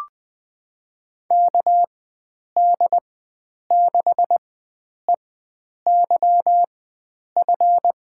Morse code, heard as a keyed tone, about 20 wpm, spaced as for 9 wpm, 700 Hz. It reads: KD6EYF
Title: USA Call Signs - 4x Character Spacing - 20wpm